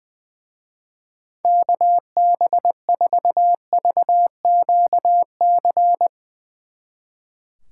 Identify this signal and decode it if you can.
Morse code, heard as a keyed tone, about 22 wpm, 700 Hz. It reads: KB4VQC
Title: Speed Racing - USA 2x3 Call Signs 20wpm